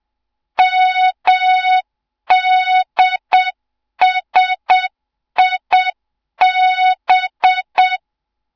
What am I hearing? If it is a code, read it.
Morse code, heard as a keyed tone, about 7 wpm, 3700 Hz. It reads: MDSIB
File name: Morze-HA2MN / hang / 07 GYAKORLAT